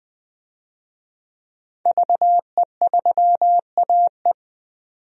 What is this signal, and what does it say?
Morse code, heard as a keyed tone, about 20 wpm, 700 Hz. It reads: VE3AE
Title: DXCC-100 Call Signs - 5 Characters - 20wpm